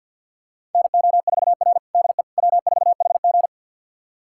Morse code, heard as a keyed tone, about 52 wpm, 700 Hz. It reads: NO4R DE W4HG
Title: USA Callsign DE Callsign - 4 Characters - 50wpm